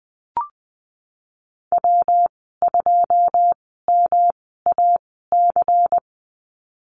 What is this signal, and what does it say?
Morse code, heard as a keyed tone, about 20 wpm, spaced as for 14 wpm, 700 Hz. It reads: W2MAC
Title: 147 Lesson 24g - Letter y - Call Signs 20wpm